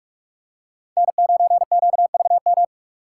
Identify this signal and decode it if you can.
Morse code, heard as a keyed tone, about 44 wpm, 700 Hz. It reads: N9QVM